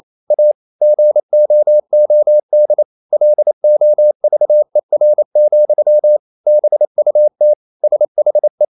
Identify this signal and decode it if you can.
Morse code, heard as a keyed tone, about 28 wpm, 600 Hz. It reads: A GOOD LOVER, BUT SHE
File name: Never 0027wpm